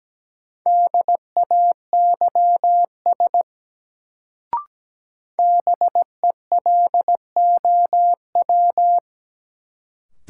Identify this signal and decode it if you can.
Morse code, heard as a keyed tone, about 17 wpm, 700 Hz. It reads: DAYS BELOW
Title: Sentences from Top 1000 Words - Review Part 5 - New Word Drill - 17wpm